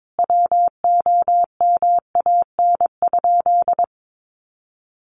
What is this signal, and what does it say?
Morse code, heard as a keyed tone, about 22 wpm, 700 Hz. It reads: WOMAN?
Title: Sentences from Top 1100 Words 22wpm